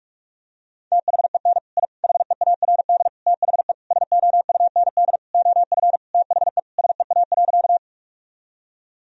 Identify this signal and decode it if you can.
Morse code, heard as a keyed tone, about 41 wpm, 700 Hz. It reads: THEN I HEARD THE SOUND OF THE SEA.
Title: Sentences from Top 1000 Words - Review Part 1 - 45wpm